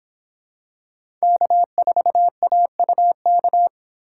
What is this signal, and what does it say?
Morse code, heard as a keyed tone, about 25 wpm, 700 Hz. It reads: K4AUK